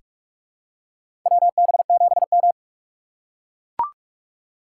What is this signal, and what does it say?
Morse code, heard as a keyed tone, about 41 wpm, 690 Hz. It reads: WB7M
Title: USA Call Signs - 4 Characters - 2x Character Spacing - 45wpm